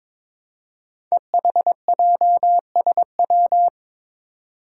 Morse code, heard as a keyed tone, about 21 wpm, 700 Hz. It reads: EHJSW
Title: Code Groups - 5 Letters - 22wpm